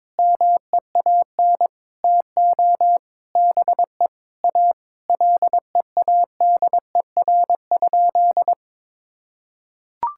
Morse code, heard as a keyed tone, about 22 wpm, 700 Hz. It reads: MEAN TO BE A LEADER?
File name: Sentences with 7-Word Phrases - 22wpm